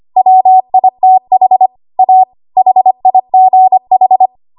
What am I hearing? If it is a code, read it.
Morse code, heard as a keyed tone, about 25 wpm, 750 Hz. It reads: WITH A HIGH
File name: Archive / 25 WPM / 25WPM